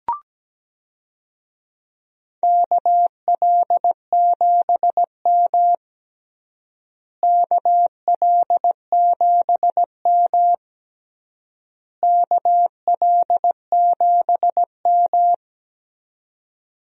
Morse code, heard as a keyed tone, about 17 wpm, 700 Hz. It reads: KL7M KL7M KL7M
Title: DXCC-100 Call Signs - 4 Characters - Repeated 3x - 17wpm